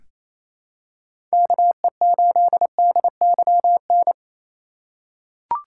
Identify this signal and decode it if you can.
Morse code, heard as a keyed tone, about 28 wpm, 700 Hz. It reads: KE8DYN
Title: USA Call Signs - 3x Character Spacing - 28wpm